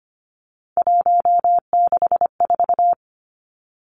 Morse code, heard as a keyed tone, about 25 wpm, 700 Hz. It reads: 164
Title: Numbers - 3-Digit 25wpm